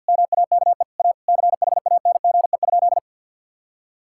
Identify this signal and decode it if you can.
Morse code, heard as a keyed tone, about 50 wpm, 700 Hz. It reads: MAKE A CHANGE?
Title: Sentences from Top 1600 Words 50wpm